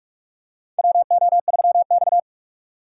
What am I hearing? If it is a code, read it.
Morse code, heard as a keyed tone, about 46 wpm, 700 Hz. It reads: WO3X
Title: USA Call Signs - 4 Characters - 4x Character Spacing - 45wpm